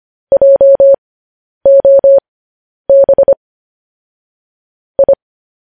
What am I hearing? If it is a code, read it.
Morse code, heard as a keyed tone, about 25 wpm, spaced as for 10 wpm, 550 Hz. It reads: JOB I
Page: news-headlines-in-morse-code-at-10-wpm